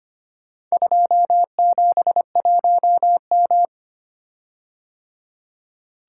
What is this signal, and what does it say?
Morse code, heard as a keyed tone, about 25 wpm, 700 Hz. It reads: 271M
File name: practice / 25wpm / Four Letter-Number Combinations - 25wpm